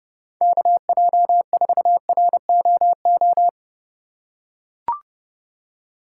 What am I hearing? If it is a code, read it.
Morse code, heard as a keyed tone, about 30 wpm, 700 Hz. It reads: KJ4ROO
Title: USA Call Signs - 6 Characters - 4x Character Spacing - 30wpm